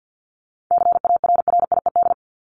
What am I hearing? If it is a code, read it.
Morse code, heard as a keyed tone, about 50 wpm, 700 Hz. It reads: CARRIED
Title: ED Suffix - Top 100 Words with Base Words 50wpm